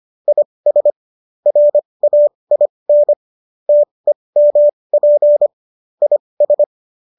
Morse code, heard as a keyed tone, about 25 wpm, spaced as for 20 wpm, 590 Hz. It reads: IS RAIN TEMP IS